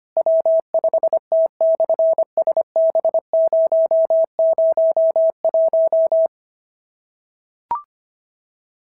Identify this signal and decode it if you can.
Morse code, heard as a keyed tone, about 25 wpm, 690 Hz. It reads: W5T/SB001
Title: SOTA QSO Practice 25wpm